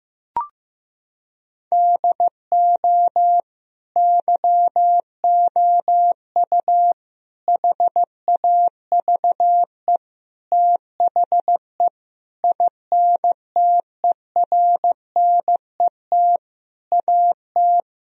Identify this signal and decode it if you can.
Morse code, heard as a keyed tone, about 15 wpm, 700 Hz. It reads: DO YOU HAVE THE INTERNET AT
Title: Sentences from Top 1000 Words - Review Part 4 - 15wpm